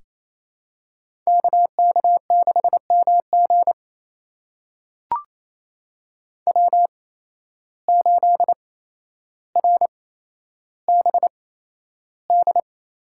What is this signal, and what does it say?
Morse code, heard as a keyed tone, about 27 wpm, spaced as for 12 wpm, 700 Hz. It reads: KK6MG W8RBD
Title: USA Call Signs - 5 Characters - 8x Character Spacing - 28wpm